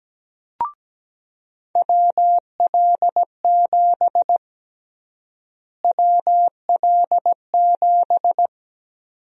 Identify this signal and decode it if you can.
Morse code, heard as a keyed tone, about 17 wpm, 700 Hz. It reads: WL7 WL7
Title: USA Call Sign Prefixes - Repeated 2x - 17wpm